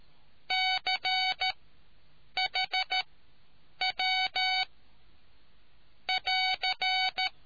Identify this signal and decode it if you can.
Morse code, heard as a keyed tone, about 13 wpm, spaced as for 7 wpm, 750 Hz. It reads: CHW <AR>